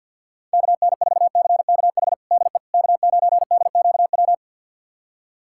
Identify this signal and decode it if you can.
Morse code, heard as a keyed tone, about 53 wpm, 700 Hz. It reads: KN4CKH DE K9DQW
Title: USA Callsign DE Callsign 50wpm